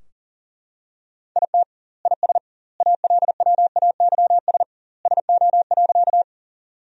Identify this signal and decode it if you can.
Morse code, heard as a keyed tone, about 40 wpm, 700 Hz. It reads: IT IS ALWAYS SO.
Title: 2x Word-Spacing - Sentences from Top 300 Words - 40wpm